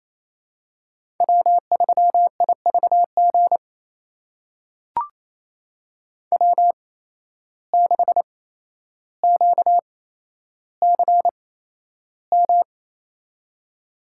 Morse code, heard as a keyed tone, about 28 wpm, spaced as for 12 wpm, 700 Hz. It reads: W3IVG W6QCM
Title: USA Call Signs - 8x Character Spacing - 28wpm